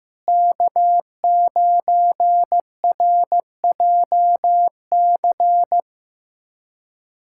Morse code, heard as a keyed tone, about 15 wpm, 700 Hz. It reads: K9RJC
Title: USA Callsign DE Callsign - 5 Characters - 15wpm